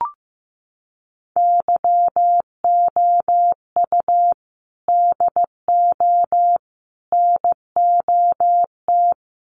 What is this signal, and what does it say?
Morse code, heard as a keyed tone, about 15 wpm, 700 Hz. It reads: YOU DO NOT
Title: Sets of 5 Words Encore 15wpm